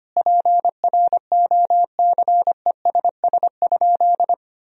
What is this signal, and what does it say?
Morse code, heard as a keyed tone, about 25 wpm, 700 Hz. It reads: PROCESS?
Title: Sentences from Top 1000 Words - Review Part 3 - 25wpm